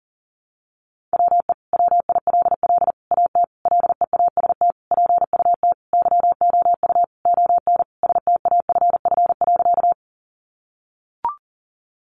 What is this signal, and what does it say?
Morse code, heard as a keyed tone, about 40 wpm, 700 Hz. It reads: WE WILL AT LEAST PUT YOU ON STAFF.
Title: Sentences from Top 1200 Words 40wpm